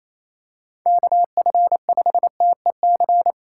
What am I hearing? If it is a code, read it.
Morse code, heard as a keyed tone, about 28 wpm, 700 Hz. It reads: KF5TEC
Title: USA Call Signs - 6 Characters - 3x Character Spacing - 28wpm